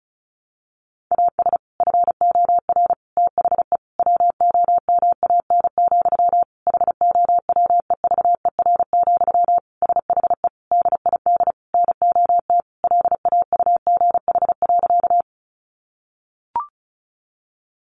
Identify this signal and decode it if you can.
Morse code, heard as a keyed tone, about 34 wpm, 700 Hz. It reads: AS FOR THE WOMAN, HOWEVER, SHE DID NOT LAUGH.
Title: Sentences from Top 1100 Words 35wpm